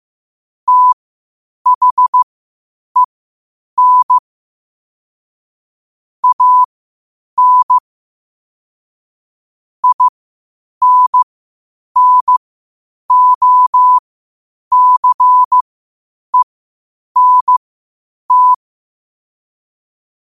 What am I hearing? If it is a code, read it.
Morse code, heard as a keyed tone, about 15 wpm, spaced as for 8 wpm, 1000 Hz. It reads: THEN AN INNOCENT
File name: qotdfiles / 8wpm